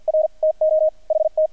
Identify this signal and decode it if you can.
Morse code, heard as a keyed tone, about 48 wpm, spaced as for 42 wpm, 630 Hz, with radio noise. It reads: R TO HT
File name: archived / CRSnet